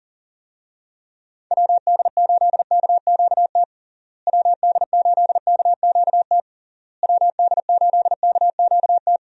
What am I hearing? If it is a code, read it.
Morse code, heard as a keyed tone, about 40 wpm, 700 Hz. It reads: WD8KQT WD8KQT WD8KQT
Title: USA Call Signs - 6 Characters - Repeated 3x - 40wpm